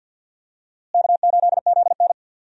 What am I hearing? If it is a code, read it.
Morse code, heard as a keyed tone, about 51 wpm, 700 Hz. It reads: K8ZN
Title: USA Call Signs - 4 Characters - Repeated 2x - 50wpm